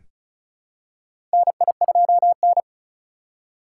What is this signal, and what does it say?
Morse code, heard as a keyed tone, about 36 wpm, 700 Hz. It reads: NI2N
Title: USA Call Signs - 4 Characters - 4x Character Spacing - 35wpm